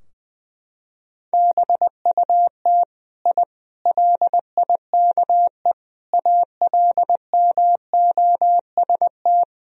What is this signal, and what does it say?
Morse code, heard as a keyed tone, about 20 wpm, 700 Hz. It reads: BUT I LIKE ALMOST
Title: Sentences from Top 1000 Words - Review Part 4 - 20wpm